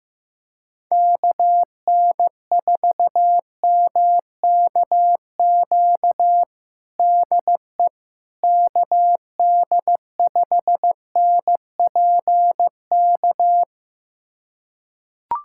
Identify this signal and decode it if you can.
Morse code, heard as a keyed tone, about 15 wpm, 700 Hz. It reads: KN4MKQ DE KD5NPK
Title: USA Callsign DE Callsign - 6 Characters - 15wpm